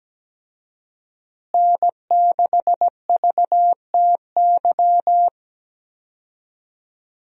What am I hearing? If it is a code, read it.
Morse code, heard as a keyed tone, about 17 wpm, 700 Hz. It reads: N6VTY
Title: USA Call Signs - 5 Characters - Repeated 3x - 17wpm